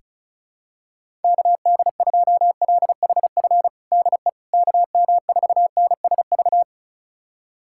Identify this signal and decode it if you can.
Morse code, heard as a keyed tone, about 36 wpm, 700 Hz. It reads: KD2LHF DE KM4NSV
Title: USA Callsign DE Callsign - 6 Characters - 35wpm